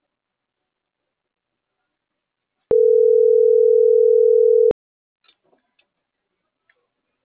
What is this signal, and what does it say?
Ringback tone (North America), 440 + 480 Hz.